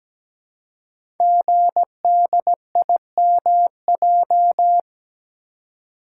Morse code, heard as a keyed tone, about 17 wpm, 700 Hz. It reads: GDIMJ
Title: Code Groups - 5 Letters - 17wpm